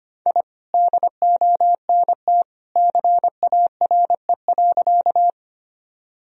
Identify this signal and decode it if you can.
Morse code, heard as a keyed tone, about 25 wpm, 700 Hz. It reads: I DONT CARE.
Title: Sentences from Top 1000 Words - Review Part 1 - 25wpm